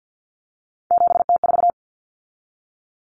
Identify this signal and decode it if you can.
Morse code, heard as a keyed tone, about 50 wpm, 700 Hz. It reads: 7T4
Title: Common Cut Numbers - 3-Digit 50wpm